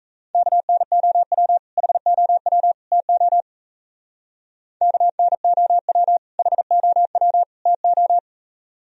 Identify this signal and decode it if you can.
Morse code, heard as a keyed tone, about 36 wpm, 700 Hz. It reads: KNOW HOW TO KNOW HOW TO